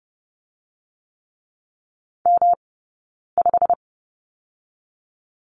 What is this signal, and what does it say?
Morse code, heard as a keyed tone, about 30 wpm, 700 Hz. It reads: M 5
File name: Lesson 19d - Letter w - Two Characters 30wpm